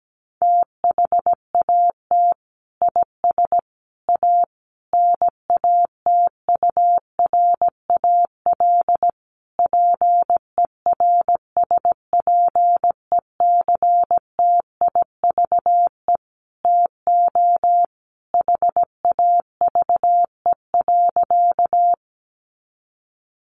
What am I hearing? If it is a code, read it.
Morse code, heard as a keyed tone, about 17 wpm, 700 Hz. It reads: THAT IS A NATURAL PERSPECTIVE TO HAVE.
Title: Sentences from Top 1200 Words 17wpm